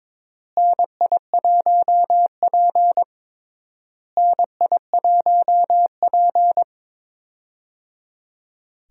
Morse code, heard as a keyed tone, about 23 wpm, 700 Hz. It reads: NI1P NI1P
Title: USA Call Signs - 4 Characters - Repeated 3x - 22wpm